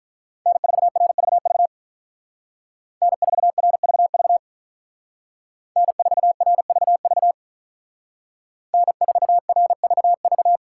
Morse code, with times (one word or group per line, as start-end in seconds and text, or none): 0.5-1.7 N4RVV
3.0-4.4 N4RVV
5.8-7.3 N4RVV
8.7-10.6 N4RVV